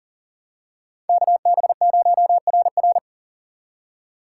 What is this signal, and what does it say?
Morse code, heard as a keyed tone, about 40 wpm, 700 Hz. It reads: KB0RR
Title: USA Call Signs - 5 Characters - 2x Character Spacing - 40wpm